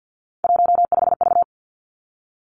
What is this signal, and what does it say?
Morse code, heard as a keyed tone, about 50 wpm, 700 Hz. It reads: 15V
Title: Three Challenging Letter-Number Combinations 50wpm